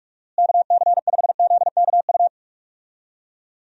Morse code, heard as a keyed tone, about 43 wpm, 700 Hz. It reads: KC5ZKU
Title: USA Call Signs - 6 Characters - Repeated 3x - 45wpm